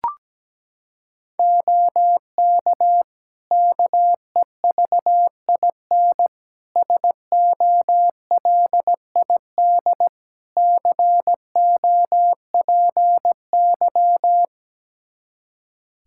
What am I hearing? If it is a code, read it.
Morse code, heard as a keyed tone, about 17 wpm, 700 Hz. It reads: OK KEVIN SOLID COPY
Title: Practice QSO - Exchange 3 - More Info - 17wpm